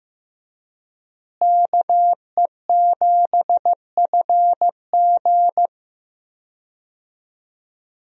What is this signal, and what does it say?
Morse code, heard as a keyed tone, about 16 wpm, 700 Hz. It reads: KE7FG
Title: USA Call Signs - 5 Characters - Repeated 2x - 15wpm